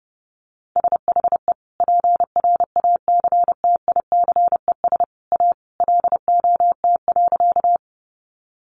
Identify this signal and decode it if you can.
Morse code, heard as a keyed tone, about 30 wpm, 700 Hz. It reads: SHE PRACTICES A LOT.
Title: Additional Easy Phrases 30wpm